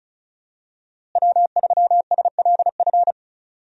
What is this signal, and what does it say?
Morse code, heard as a keyed tone, about 34 wpm, 700 Hz. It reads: W3SLF
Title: USA Call Signs - 5 Characters - Repeated 3x - 35wpm